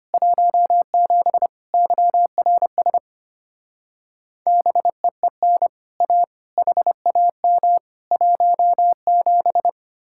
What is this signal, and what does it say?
Morse code, heard as a keyed tone, about 33 wpm, 700 Hz. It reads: 17 YRS BEEN A HAM 17